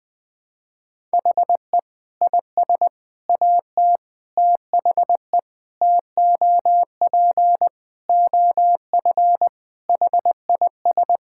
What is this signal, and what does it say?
Morse code, heard as a keyed tone, about 20 wpm, 700 Hz. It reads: HE IS AT THE TOP OF HIS
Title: Sentences with 4-Word Phrases - 20wpm